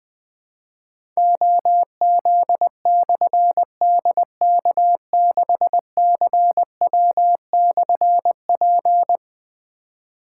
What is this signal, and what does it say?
Morse code, heard as a keyed tone, about 20 wpm, 700 Hz. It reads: OZ/DK6CW/P
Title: Difficult Call Signs 20wpm